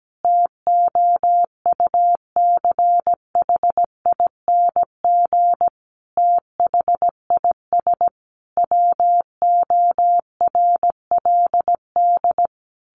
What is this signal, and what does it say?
Morse code, heard as a keyed tone, about 17 wpm, 700 Hz. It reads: TOUCHING THIS WORLD
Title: A Sound Of Thunder 17wpm